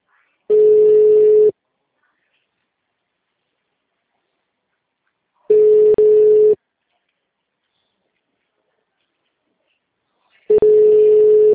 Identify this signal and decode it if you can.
Ringback tone (Europe, 425 Hz).
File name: out- -2001- - - .11594